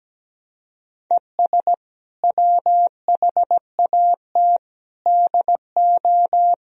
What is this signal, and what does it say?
Morse code, heard as a keyed tone, about 17 wpm, 700 Hz. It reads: ES WHAT DO
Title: Sentences from Top 100 Words and Abbreviations 17wpm